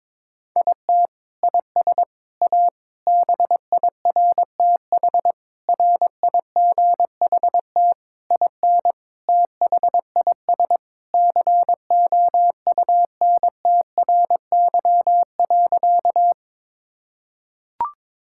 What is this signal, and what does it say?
Morse code, heard as a keyed tone, about 22 wpm, 700 Hz. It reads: IT IS A BIRTH RIGHT IN THIS COUNTRY.
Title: Sentences from Top 1000 Words - Review Part 5 - 22wpm